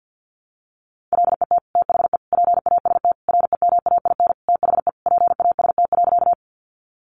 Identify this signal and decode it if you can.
Morse code, heard as a keyed tone, about 50 wpm, 700 Hz. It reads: LET THE PAST REMAIN THE PAST.
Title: Sentences from Top 1200 Words 50wpm